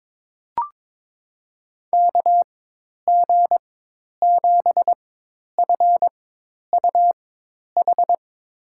Morse code, heard as a keyed tone, about 22 wpm, spaced as for 10 wpm, 700 Hz. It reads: KG7FUH
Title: USA Call Signs - 6 Characters - 4x Character Spacing - 22wpm